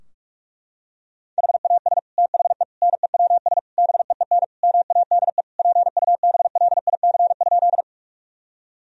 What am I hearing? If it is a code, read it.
Morse code, heard as a keyed tone, about 41 wpm, 700 Hz. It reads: HAS THE NEWS BEEN MADE PUBLIC?
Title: Sentences from Top 1000 Words - Review Part 3 - 45wpm